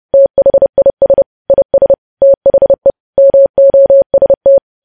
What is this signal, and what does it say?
Morse code, heard as a keyed tone, about 30 wpm, 550 Hz. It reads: THIS IS THE MOST
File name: podcast / News headlines-1x-30x30